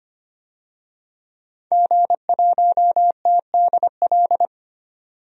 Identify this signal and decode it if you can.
Morse code, heard as a keyed tone, about 25 wpm, 700 Hz. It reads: G1TDL
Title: DXCC-100 Call Signs - 5 Characters - 25wpm